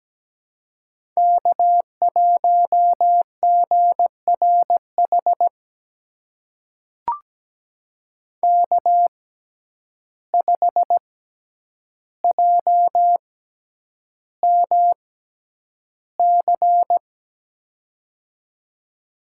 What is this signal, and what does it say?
Morse code, heard as a keyed tone, about 17 wpm, spaced as for 9 wpm, 700 Hz. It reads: K1GRH K5JMC
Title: USA Call Signs - 6x Character Spacing - 17wpm